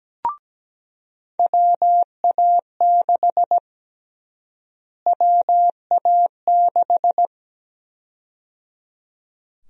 Morse code, heard as a keyed tone, about 17 wpm, 700 Hz. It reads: WA6 WA6
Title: USA Call Sign Prefixes - Repeated 2x - 17wpm